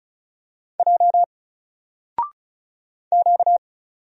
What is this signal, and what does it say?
Morse code, heard as a keyed tone, about 34 wpm, 700 Hz. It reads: J Q
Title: Single Letter-Number Alt 35wpm